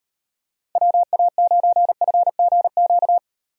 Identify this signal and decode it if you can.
Morse code, heard as a keyed tone, about 38 wpm, 700 Hz. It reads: WA9FGQ